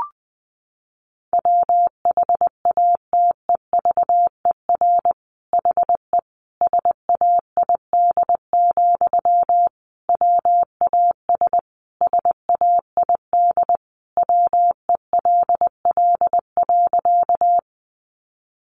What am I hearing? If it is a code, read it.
Morse code, heard as a keyed tone, about 20 wpm, 700 Hz. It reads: WHATEVER HE SAID, WAS SAID WELL.
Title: Sentences from Top 1100 Words 20wpm